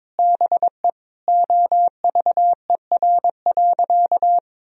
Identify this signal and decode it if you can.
Morse code, heard as a keyed tone, about 22 wpm, 700 Hz. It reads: BE OVER.
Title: Sentences from Top 1000 Words - Review Part 2 - 22wpm